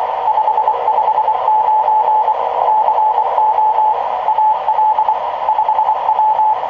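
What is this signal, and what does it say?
Morse code, heard as a keyed tone, about 23 wpm, 860 Hz, with radio noise. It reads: 559 CO AL 5W
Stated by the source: QRP fox hunt: fox KØFRP (Al Dawkins, Colorado QRP Club #32) working hounds AA4LR and N0OUR.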